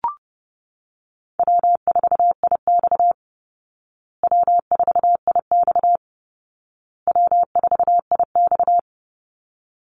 Morse code, heard as a keyed tone, about 30 wpm, 700 Hz. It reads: W4IX W4IX W4IX
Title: Call Signs Repeated 2x - 30wpm